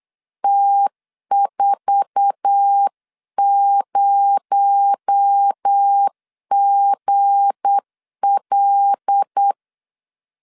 Morse code, heard as a keyed tone, about 8 wpm, 800 Hz. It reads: T40GL
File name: LEZIONI / 10 SIGNS / 10B INT 8wpm 3534